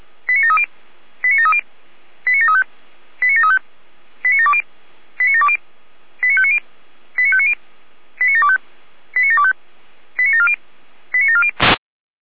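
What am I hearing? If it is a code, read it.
Five-tone selective call (ZVEI), heard as quick runs of five tones, tones of about 70 ms. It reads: 89730 89730 89735 89735 89720 89720 89590 89590 89725 89725 89740 89740
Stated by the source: ZVEI five-tone selective call received by SDR.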